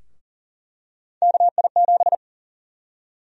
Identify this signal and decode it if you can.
Morse code, heard as a keyed tone, about 40 wpm, 700 Hz. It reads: KI7
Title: USA Call Sign Prefixes - 40wpm